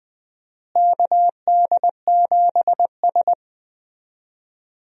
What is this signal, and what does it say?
Morse code, heard as a keyed tone, about 20 wpm, 700 Hz. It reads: KD7S
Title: USA Call Signs - 4 Characters - Repeated 3x - 20wpm